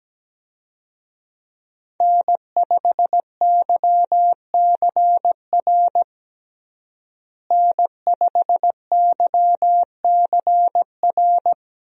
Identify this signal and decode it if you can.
Morse code, heard as a keyed tone, about 17 wpm, 700 Hz. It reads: N5YCR N5YCR
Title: USA Call Signs - 5 Characters - Repeated 2x - 17wpm